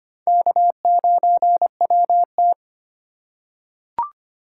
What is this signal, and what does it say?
Morse code, heard as a keyed tone, about 25 wpm, 700 Hz. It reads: K9WT
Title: USA Call Signs - 4 Characters - 3x Character Spacing - 25wpm